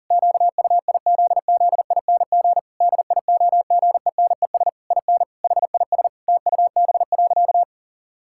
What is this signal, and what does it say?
Morse code, heard as a keyed tone, about 40 wpm, 700 Hz. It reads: QUIZZING DIOGENES IN HIS TUB.
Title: Pangram Challenge v2 - 40wpm